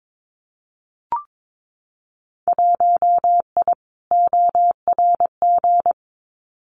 Morse code, heard as a keyed tone, about 22 wpm, 700 Hz. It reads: 1I ORG
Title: Winter Field Day Exchanges Encore 22wpm